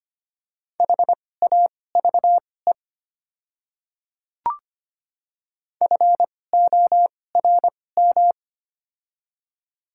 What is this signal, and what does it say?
Morse code, heard as a keyed tone, about 25 wpm, spaced as for 18 wpm, 700 Hz. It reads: HAVE FORM
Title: 2x Character Spacing - Top 200 Words - 25wpm